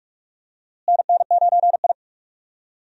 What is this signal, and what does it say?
Morse code, heard as a keyed tone, about 42 wpm, 700 Hz. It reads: NN9I
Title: USA Call Signs - 4 Characters - Repeated 2x - 45wpm